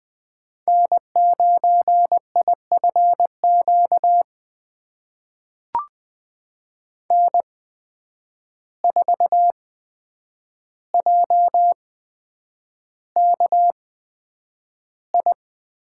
Morse code, heard as a keyed tone, about 20 wpm, spaced as for 9 wpm, 700 Hz. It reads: N9IFQ N4JKI
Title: USA Call Signs - 5 Characters - 8x Character Spacing - 20wpm